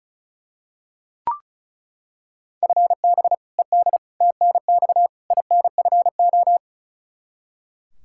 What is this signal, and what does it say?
Morse code, heard as a keyed tone, about 36 wpm, 700 Hz. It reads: FB ED TNX INFO
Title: K1USN SST QSO Practice 35wpm